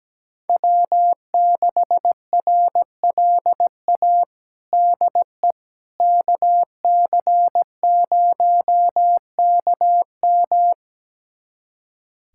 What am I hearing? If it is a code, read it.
Morse code, heard as a keyed tone, about 17 wpm, 700 Hz. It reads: W6RLA DE KC0KM
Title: USA Callsign DE Callsign - 5 Characters - 17wpm